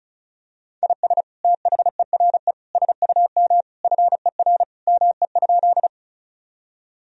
Morse code, heard as a keyed tone, about 34 wpm, 700 Hz. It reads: IS THERE SUM FER ME?
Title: Sentences from Top 200 Words and Abbreviations 35wpm